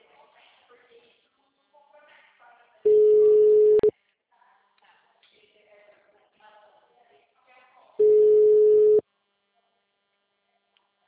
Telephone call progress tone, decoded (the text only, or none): ringback tone